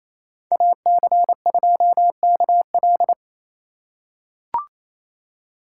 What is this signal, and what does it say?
Morse code, heard as a keyed tone, about 28 wpm, 700 Hz. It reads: AC2KL
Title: USA Call Signs - 2x Character Spacing - 28wpm